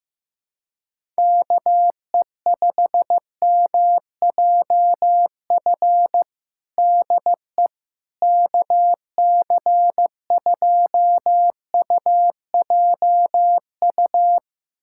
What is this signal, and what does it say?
Morse code, heard as a keyed tone, about 15 wpm, 700 Hz. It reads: KE5MJF DE KC2UJU
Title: USA Callsign DE Callsign - 6 Characters - 15wpm